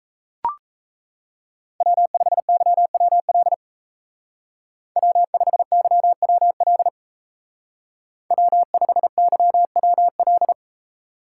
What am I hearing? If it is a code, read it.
Morse code, heard as a keyed tone, about 35 wpm, 700 Hz. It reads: W5YWL W5YWL W5YWL
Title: Speed Racing - USA 1x3 Call Signs 28wpm